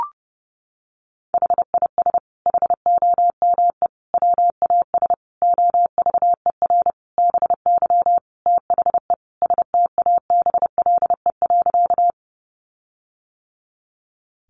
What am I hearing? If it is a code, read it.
Morse code, heard as a keyed tone, about 30 wpm, 700 Hz. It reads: HIS HOME WAS OVER BY THE STABLE.
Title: Sentences from Top 1500 Words 30wpm